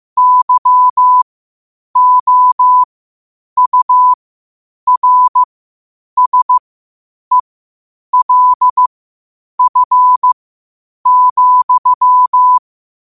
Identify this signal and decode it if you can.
Morse code, heard as a keyed tone, about 15 wpm, spaced as for 8 wpm, 1000 Hz. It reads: YOURSELF,
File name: qotdfiles / 8wpm